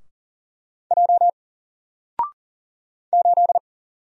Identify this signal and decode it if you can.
Morse code, heard as a keyed tone, about 39 wpm, 700 Hz. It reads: J 8
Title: Single Character (Letter:Number:Punctuation:Prosign) 40wpm